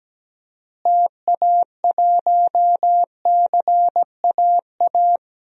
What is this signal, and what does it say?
Morse code, heard as a keyed tone, about 17 wpm, 700 Hz. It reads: TA1CAA
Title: DXCC-100 Call Signs - 6 Characters - Repeated 2x - 17wpm